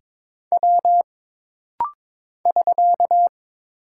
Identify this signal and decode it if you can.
Morse code, heard as a keyed tone, about 22 wpm, 700 Hz. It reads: W <SK>
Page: This rapid-fire timing is geared to push your ICR (Instant Character Recognition) skills.